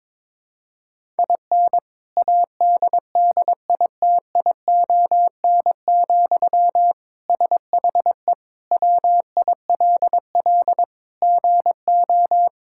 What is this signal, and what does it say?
Morse code, heard as a keyed tone, about 22 wpm, 700 Hz. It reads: IN ADDITION, SHE WILL GO
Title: Sentences from Top 1000 Words - Review Part 6 - 22wpm